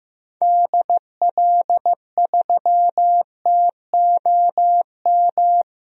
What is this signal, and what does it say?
Morse code, heard as a keyed tone, about 15 wpm, 700 Hz. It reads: DL3TOM
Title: DXCC-100 Call Signs - 6 Characters - 15wpm